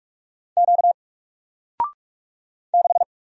Morse code, heard as a keyed tone, about 44 wpm, 700 Hz. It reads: Q 6
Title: Single Letter-Number Alt 45wpm